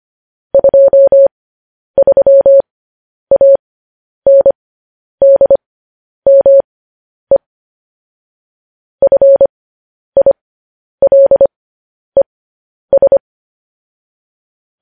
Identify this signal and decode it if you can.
Morse code, heard as a keyed tone, about 25 wpm, spaced as for 10 wpm, 550 Hz. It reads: 23ANDME FILES
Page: news-headlines-in-morse-code-at-10-wpm